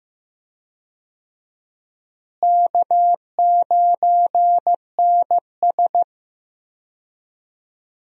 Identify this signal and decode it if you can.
Morse code, heard as a keyed tone, about 15 wpm, 700 Hz. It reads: K9NS
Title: USA Call Signs - 4 Characters - Repeated 3x - 15wpm